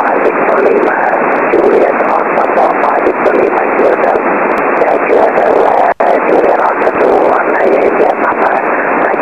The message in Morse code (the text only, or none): none